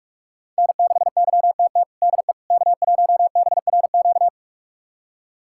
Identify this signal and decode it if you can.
Morse code, heard as a keyed tone, about 46 wpm, 700 Hz. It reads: N6YTT DE K1BRQ
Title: USA Callsign DE Callsign 45wpm